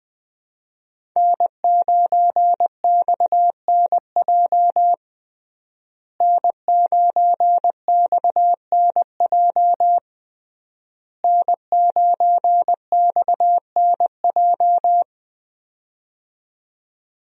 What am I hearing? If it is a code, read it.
Morse code, heard as a keyed tone, about 21 wpm, 700 Hz. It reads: N9XNJ N9XNJ N9XNJ
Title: USA Call Signs - 5 Characters - Repeated 3x - 20wpm